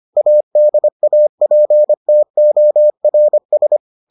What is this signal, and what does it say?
Morse code, heard as a keyed tone, about 25 wpm, 600 Hz. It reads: ADAPTORS